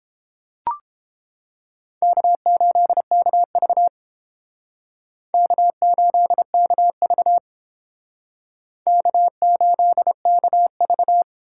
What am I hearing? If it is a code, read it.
Morse code, heard as a keyed tone, about 32 wpm, 700 Hz. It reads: K8KV K8KV K8KV